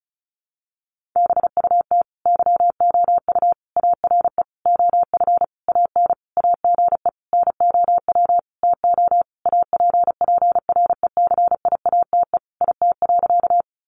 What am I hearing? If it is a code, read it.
Morse code, heard as a keyed tone, about 34 wpm, 700 Hz. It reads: BUT YOU ARE OF AN AGE NOW TO APPRECIATE IT.
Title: Sentences from Top 1300 Words 35wpm